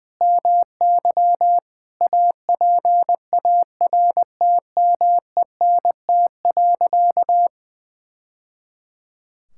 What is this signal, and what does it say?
Morse code, heard as a keyed tone, about 21 wpm, 700 Hz. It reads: MY APARTMENT.
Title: Sentences from Top 1000 Words - Review Part 6 - 20wpm